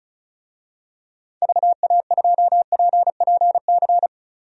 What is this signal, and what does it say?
Morse code, heard as a keyed tone, about 36 wpm, 700 Hz. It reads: VA2PPC
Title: DXCC-100 Call Signs - 6 Characters - 35wpm